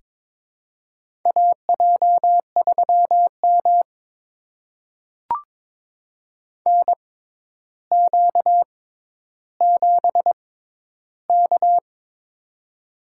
Morse code, heard as a keyed tone, about 22 wpm, spaced as for 11 wpm, 700 Hz. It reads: AJ3M NQ7K
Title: USA Call Signs - 4 Characters - 6x Character Spacing - 22wpm